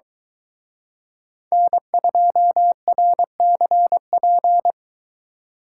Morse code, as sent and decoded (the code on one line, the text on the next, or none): -. ..--- .-. -.-. .--.
N2RCP